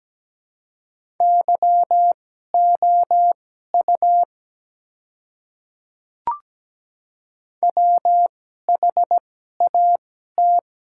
Morse code, heard as a keyed tone, about 17 wpm, spaced as for 12 wpm, 700 Hz. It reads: YOU WHAT
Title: 2x Character Spacing - Top 100 Words - 17wpm